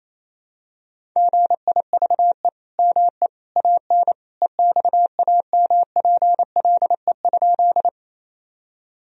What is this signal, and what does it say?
Morse code, heard as a keyed tone, about 28 wpm, 700 Hz. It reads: GIVE ME AN EXAMPLE?
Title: Sentences from Top 1000 Words - Review Part 1 - 28wpm